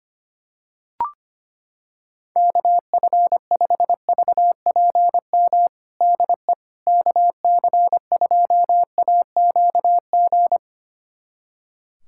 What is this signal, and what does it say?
Morse code, heard as a keyed tone, about 25 wpm, 700 Hz. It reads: KF5VPM DE KC2AQG
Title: USA Callsign DE Callsign - 6 Characters - 25wpm